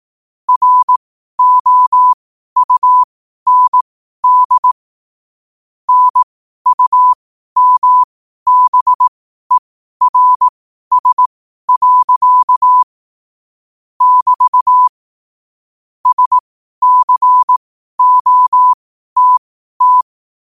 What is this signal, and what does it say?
Morse code, heard as a keyed tone, about 18 wpm, spaced as for 12 wpm, 1000 Hz. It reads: ROUND NUMBERS. <BT> SCOTT
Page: 12 wpm morse code quotes